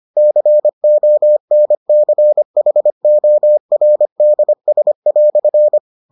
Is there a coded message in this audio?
Yes: Morse code from 0.2 to 5.8 s, heard as a keyed tone, about 25 wpm, 600 Hz. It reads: CONCHORDS"